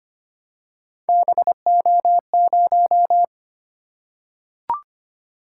Morse code, heard as a keyed tone, about 25 wpm, 700 Hz. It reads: BO0
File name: Three Letter-Number Combinations - 25wpm